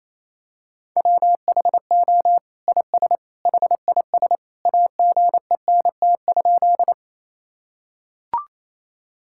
Morse code, heard as a keyed tone, about 28 wpm, 700 Hz. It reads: WHO IS HIS AGENT?
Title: Sentences from Top 1600 Words 28wpm